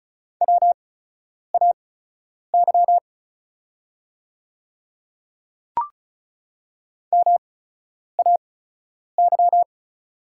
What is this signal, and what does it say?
Morse code, heard as a keyed tone, about 36 wpm, spaced as for 10 wpm, 700 Hz. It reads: WAY MAY
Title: 8x Character Spacing - Top 100 Words - 35wpm